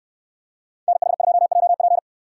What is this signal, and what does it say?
Morse code, heard as a keyed tone, about 69 wpm, 700 Hz. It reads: NH2JP